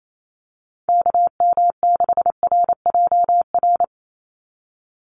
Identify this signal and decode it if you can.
Morse code, heard as a keyed tone, about 28 wpm, 700 Hz. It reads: KM6RJR
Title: USA Call Signs 28wpm